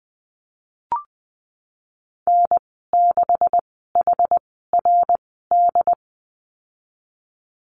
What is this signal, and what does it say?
Morse code, heard as a keyed tone, about 19 wpm, spaced as for 14 wpm, 700 Hz. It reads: N6HRD
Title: 154 Lesson 25g - Letter p - Call Signs 20wpm